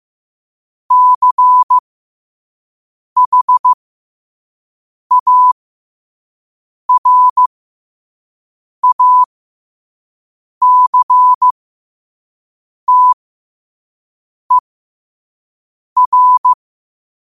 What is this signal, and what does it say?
Morse code, heard as a keyed tone, about 15 wpm, spaced as for 5 wpm, 1000 Hz. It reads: CHARACTER
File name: qotdfiles / 5wpm